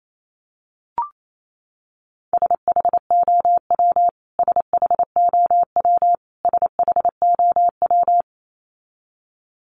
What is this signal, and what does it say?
Morse code, heard as a keyed tone, about 28 wpm, 700 Hz. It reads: SHOW SHOW SHOW
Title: Top 200 Words Repeated 28wpm